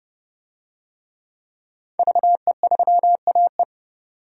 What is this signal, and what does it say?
Morse code, heard as a keyed tone, about 30 wpm, 700 Hz. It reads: VE3AE
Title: DXCC-100 Call Signs - 5 Characters - 30wpm